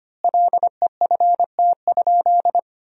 Morse code, heard as a keyed tone, about 25 wpm, 700 Hz. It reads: LEFT?
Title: 3x Word-Spacing - Sentences from Top 300 Words - 25wpm